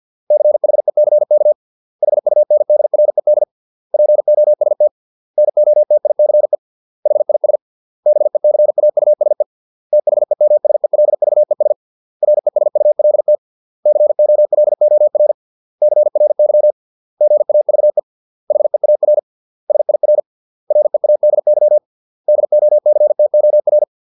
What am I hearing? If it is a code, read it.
Morse code, heard as a keyed tone, about 50 wpm, 600 Hz. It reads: CHECK HUNDRED POST NOTICE HIS BECAUSE THEMSELVES RESULT COLOR CRY GAVE HEAR SIR READY DOCTOR